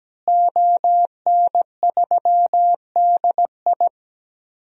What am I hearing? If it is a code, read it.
Morse code, heard as a keyed tone, about 17 wpm, 700 Hz. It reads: ON3DI
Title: DXCC-100 Call Signs - 5 Characters - Repeated 3x - 17wpm